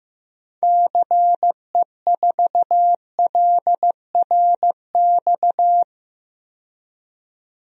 Morse code, heard as a keyed tone, about 15 wpm, 700 Hz. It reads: CE4LRX
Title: DXCC-100 Call Signs - 6 Characters - 15wpm